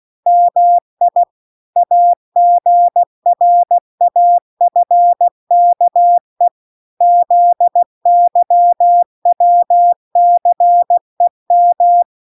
Morse code, heard as a keyed tone, about 16 wpm, 700 Hz. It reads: MI AGRAFKE ZYWCEM